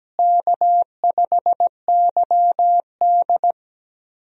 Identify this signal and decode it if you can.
Morse code, heard as a keyed tone, about 17 wpm, 700 Hz. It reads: K5YD
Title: USA Call Signs - 4 Characters - Repeated 3x - 17wpm